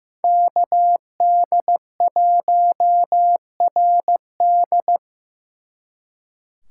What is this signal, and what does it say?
Morse code, heard as a keyed tone, about 18 wpm, 700 Hz. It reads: KD1RD
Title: Speed Racing - USA 2x2 Call Signs 15wpm